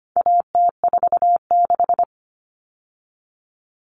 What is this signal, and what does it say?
Morse code, heard as a keyed tone, about 25 wpm, 700 Hz. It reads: AT46
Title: All Cut Numbers - 4-Digit 25wpm